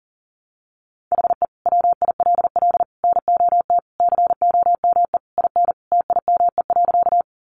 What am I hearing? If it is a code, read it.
Morse code, heard as a keyed tone, about 40 wpm, 700 Hz. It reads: HE WILL NOT COME IN TIME.
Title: Sentences from Top 600 Words 40wpm